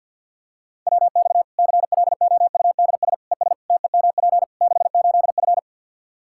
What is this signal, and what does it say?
Morse code, heard as a keyed tone, about 52 wpm, 700 Hz. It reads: WX CLOUDS ES TEMP 68F
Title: Practice QSO - Exchange 2 - Info - 50wpm